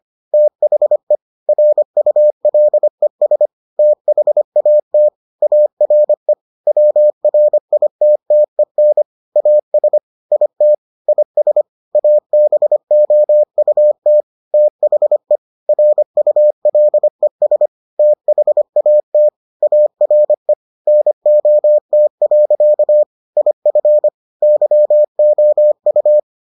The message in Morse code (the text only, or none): THE RULES THAT ARE WRITTEN AS IT IS ABOUT THE RULES THAT ARE NOT. IF YOU